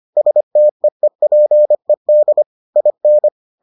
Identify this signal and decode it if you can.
Morse code, heard as a keyed tone, about 25 wpm, 600 Hz. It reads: STEEPED IN